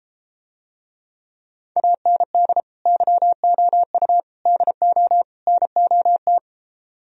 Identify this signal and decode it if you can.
Morse code, heard as a keyed tone, about 33 wpm, 700 Hz. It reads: AND YOU DO NOT